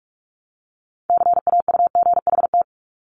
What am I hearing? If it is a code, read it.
Morse code, heard as a keyed tone, about 46 wpm, 700 Hz. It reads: CAUGHT
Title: Multi-letter Phonemes 45wpm